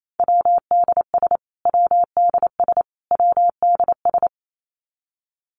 Morse code, heard as a keyed tone, about 28 wpm, 700 Hz. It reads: WDS WDS WDS
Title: Abbreviations Repeated 3x - 28wpm